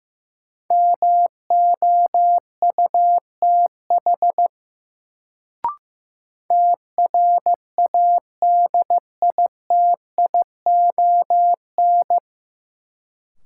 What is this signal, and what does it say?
Morse code, heard as a keyed tone, about 15 wpm, 700 Hz. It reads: MOUTH TRADITION
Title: Sentences from Top 1500 Words - New Word Drill - 15wpm